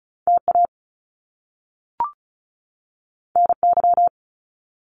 Morse code, heard as a keyed tone, about 36 wpm, 700 Hz. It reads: TA NY
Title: Top Two Letter Combinations in Words 35wpm